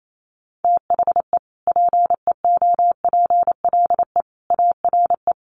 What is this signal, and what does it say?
Morse code, heard as a keyed tone, about 28 wpm, 700 Hz. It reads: THE PEOPLE ARE
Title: Sentences from Top 1400 Words 28wpm